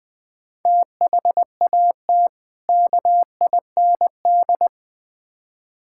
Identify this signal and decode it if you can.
Morse code, heard as a keyed tone, about 20 wpm, 700 Hz. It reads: THAT KIND
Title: Build A Phrase - 3-Word Phrases - 20wpm